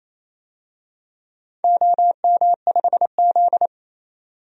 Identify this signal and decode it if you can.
Morse code, heard as a keyed tone, about 28 wpm, 700 Hz. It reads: OM5Z
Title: DXCC-100 Call Signs - 4 Characters - 28wpm